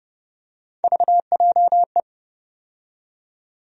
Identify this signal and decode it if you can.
Morse code, heard as a keyed tone, about 30 wpm, 700 Hz. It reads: VJE
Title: USA Call Sign Postfixes - 3 Letters 30wpm